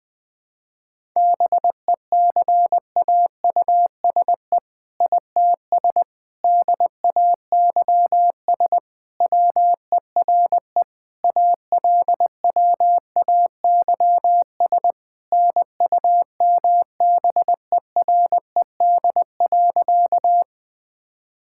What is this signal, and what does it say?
Morse code, heard as a keyed tone, about 21 wpm, 700 Hz. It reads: BECAUSE ITS DAYS WERE ALWAYS NUMBERED.
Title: All-Age Appropriate Jokes - 20wpm